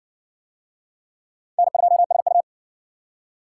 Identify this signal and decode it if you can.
Morse code, heard as a keyed tone, about 54 wpm, 700 Hz. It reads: N2SU